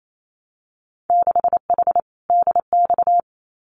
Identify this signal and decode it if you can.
Morse code, heard as a keyed tone, about 28 wpm, 700 Hz. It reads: 6H DX
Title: Winter Field Day Exchanges Encore 28wpm